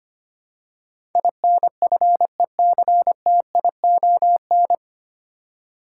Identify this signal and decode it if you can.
Morse code, heard as a keyed tone, about 25 wpm, 700 Hz. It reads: INFECTION
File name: Nine Letter Words - 25wpm